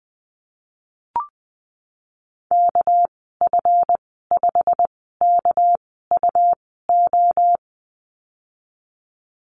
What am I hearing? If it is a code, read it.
Morse code, heard as a keyed tone, about 20 wpm, spaced as for 15 wpm, 700 Hz. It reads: KF5KUO
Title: 213 Lesson 34g - Number 8 - Call Signs 20wpm